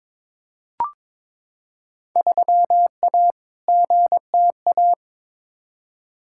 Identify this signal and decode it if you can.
Morse code, heard as a keyed tone, about 22 wpm, 700 Hz. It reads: 3A GTA
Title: Field Day Exchanges - 22wpm